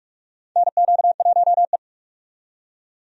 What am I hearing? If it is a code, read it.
Morse code, heard as a keyed tone, about 46 wpm, 700 Hz. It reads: NQ1E
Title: USA Call Signs - 4 Characters - 45wpm